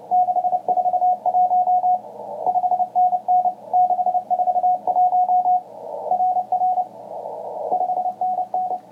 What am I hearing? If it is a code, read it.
Morse code, heard as a keyed tone, about 29 wpm, 730 Hz, with radio noise. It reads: 641 5NN 641 RR 5NN